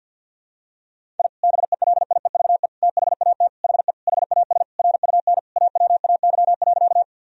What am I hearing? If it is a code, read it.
Morse code, heard as a keyed tone, about 48 wpm, 700 Hz. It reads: I BELIEVE THAT HE HAS RUN AWAY.
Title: Sentences from Top 1000 Words - Review Part 1 - 50wpm